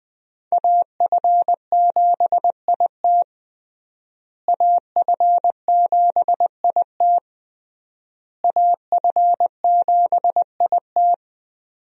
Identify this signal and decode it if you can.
Morse code, heard as a keyed tone, about 20 wpm, 700 Hz. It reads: AF7IT AF7IT AF7IT
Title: USA Call Signs - 5 Characters - Repeated 3x - 20wpm